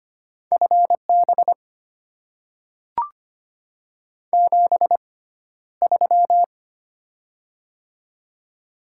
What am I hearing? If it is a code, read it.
Morse code, heard as a keyed tone, about 25 wpm, spaced as for 13 wpm, 700 Hz. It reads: FB 73
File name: Top 20 QSO Elements - 6x Character Spacing - 25wpm